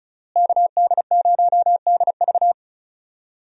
Morse code, heard as a keyed tone, about 36 wpm, 700 Hz. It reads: KD0DV
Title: USA Call Signs - 5 Characters - Repeated 3x - 35wpm